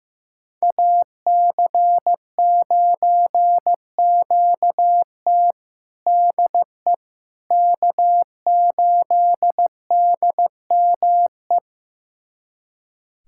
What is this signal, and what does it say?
Morse code, heard as a keyed tone, about 15 wpm, 700 Hz. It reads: AC9QT DE K8DME
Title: USA Callsign DE Callsign - 5 Characters - 15wpm